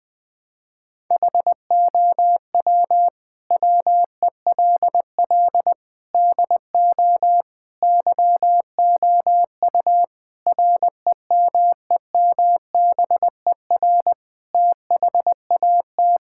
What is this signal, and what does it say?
Morse code, heard as a keyed tone, about 20 wpm, 700 Hz. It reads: HOW WELL DO YOU REMEMBER THAT
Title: Sentences from Top 1000 Words - Review Part 5 - 20wpm